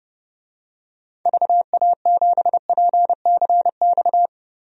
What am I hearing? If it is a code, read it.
Morse code, heard as a keyed tone, about 30 wpm, 700 Hz. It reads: VA7PCX
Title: DXCC-100 Call Signs - 6 Characters - Repeated 3x - 30wpm